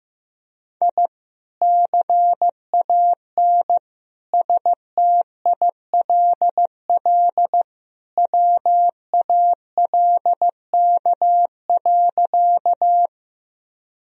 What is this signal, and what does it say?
Morse code, heard as a keyed tone, about 15 wpm, 700 Hz. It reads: I CAN STILL WALK.
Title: Sentences from Top 1000 Words - Review Part 1 - 15wpm